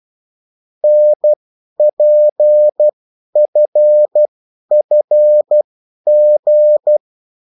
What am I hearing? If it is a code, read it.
Morse code, heard as a keyed tone, about 12 wpm, spaced as for 10 wpm, 600 Hz. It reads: NPFFG
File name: morse mp3 / revision9-vitesse 10 mots